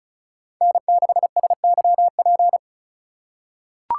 Morse code, heard as a keyed tone, about 36 wpm, 690 Hz. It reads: N6SYP
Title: USA Call Signs - 5 Characters - 3x Character Spacing - 35wpm